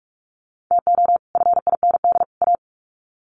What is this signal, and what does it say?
Morse code, heard as a keyed tone, about 44 wpm, 700 Hz. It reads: TO FIND A
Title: Sets of 3 Words 45wpm